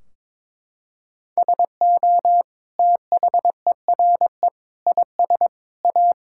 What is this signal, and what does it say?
Morse code, heard as a keyed tone, about 22 wpm, 700 Hz. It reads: SO THERE IS A
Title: Sentences from Top 1600 Words 22wpm